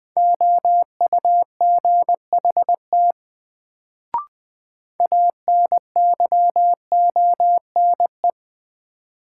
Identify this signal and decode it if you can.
Morse code, heard as a keyed tone, about 20 wpm, 700 Hz. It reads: OUGHT ANYONE
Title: Sets of 5 Words Encore - New Word Drill - 20wpm